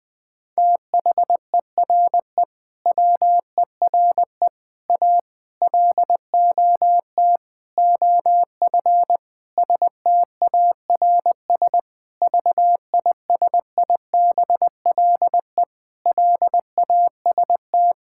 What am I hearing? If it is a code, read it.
Morse code, heard as a keyed tone, about 21 wpm, 700 Hz. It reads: THERE WERE A LOT OF STARS VISIBLE LAST
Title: Sentences with 5-Word Phrases - 20wpm